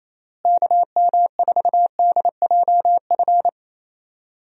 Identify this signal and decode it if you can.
Morse code, heard as a keyed tone, about 28 wpm, 700 Hz. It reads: KM4DJF
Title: USA Call Signs - 6 Characters - Repeated 2x - 28wpm